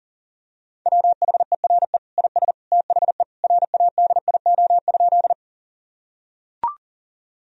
Morse code, heard as a keyed tone, about 40 wpm, 700 Hz. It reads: WHERE IS THE RADIO?
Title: Sentences from Top 1000 Words - Review Part 3 - 40wpm